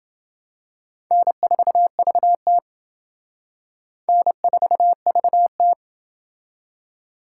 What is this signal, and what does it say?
Morse code, heard as a keyed tone, about 28 wpm, 700 Hz. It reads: N4VT N4VT